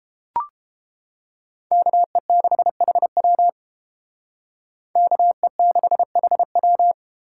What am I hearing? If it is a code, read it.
Morse code, heard as a keyed tone, about 31 wpm, 700 Hz. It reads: KE6HW KE6HW